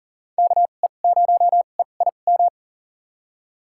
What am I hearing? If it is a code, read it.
Morse code, heard as a keyed tone, about 37 wpm, spaced as for 28 wpm, 700 Hz. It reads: KE0EIM
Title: USA Call Signs - 2x Character Spacing - 40wpm